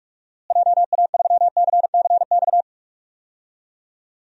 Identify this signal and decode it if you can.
Morse code, heard as a keyed tone, about 47 wpm, 700 Hz. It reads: JA3CCX
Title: DXCC-100 Call Signs - 6 Characters - Repeated 3x - 45wpm